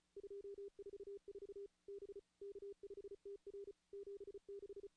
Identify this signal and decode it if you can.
Morse code, heard as a keyed tone, about 34 wpm, 400 Hz. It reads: 244 B K5TR 76
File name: contests / CW-SS-k5tr- -2300